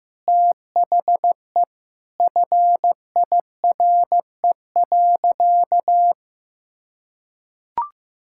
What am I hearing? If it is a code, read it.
Morse code, heard as a keyed tone, about 15 wpm, 700 Hz. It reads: THE FIRE.
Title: Sentences from Top 1600 Words 15wpm